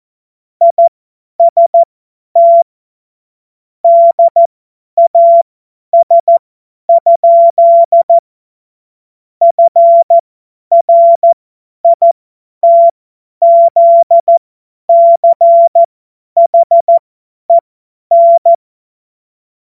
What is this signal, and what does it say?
Morse code, heard as a keyed tone, about 14 wpm, spaced as for 10 wpm, 680 Hz. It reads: IST DAS? FRITZCHEN
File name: Morsekurs / Witze / Witz 17. August 2021, 50 F70